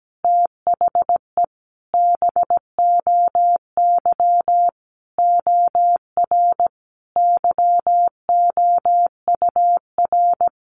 Morse code, heard as a keyed tone, about 17 wpm, 700 Hz. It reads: THE BOY OR YOUR
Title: Sentences from Top 1500 Words 17wpm